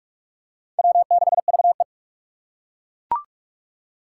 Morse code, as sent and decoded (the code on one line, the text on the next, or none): .-- -.... ...- .
W6VE